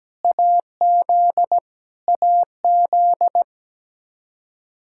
Morse code, heard as a keyed tone, about 17 wpm, 700 Hz. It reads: AZ AZ
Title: Practice QSO - Exchange 1 - Beginning - 17wpm